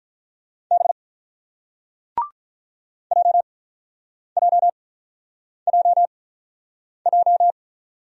Morse code, heard as a keyed tone, about 48 wpm, spaced as for 9 wpm, 700 Hz. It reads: B JJJJ